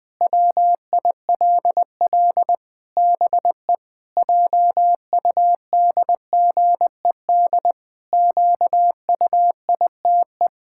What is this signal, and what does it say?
Morse code, heard as a keyed tone, about 21 wpm, 700 Hz. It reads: WILL BE JUDGED QUITE
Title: Pangram Challenge v2 - 20wpm